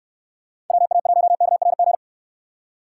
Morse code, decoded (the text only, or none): WA1KRG